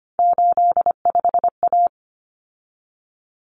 Morse code, heard as a keyed tone, about 25 wpm, 700 Hz. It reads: 85A
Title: Common Cut Numbers - 3-Digit 25wpm